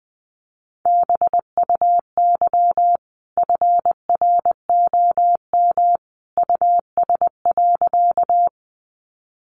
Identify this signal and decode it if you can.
Morse code, heard as a keyed tone, about 20 wpm, 700 Hz. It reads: BUY FROM US.
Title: Sentences from Top 700 Words 20wpm